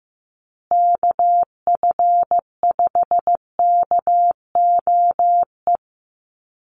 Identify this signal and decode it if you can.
Morse code, heard as a keyed tone, about 15 wpm, 700 Hz. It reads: KF5KOE
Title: USA Call Signs Repeated 3x 15wpm